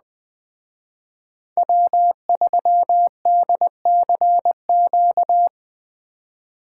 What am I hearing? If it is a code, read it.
Morse code, heard as a keyed tone, about 20 wpm, 700 Hz. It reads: W3DCQ
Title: Speed Racing - USA 1x3 Call Signs 17wpm